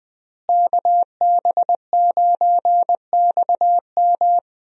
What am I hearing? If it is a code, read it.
Morse code, heard as a keyed tone, about 20 wpm, 700 Hz. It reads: KB9XM
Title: USA Call Signs - 5 Characters - Repeated 3x - 20wpm